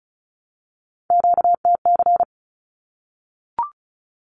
Morse code, heard as a keyed tone, about 34 wpm, 700 Hz. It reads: QTC
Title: Q-Codes 35wpm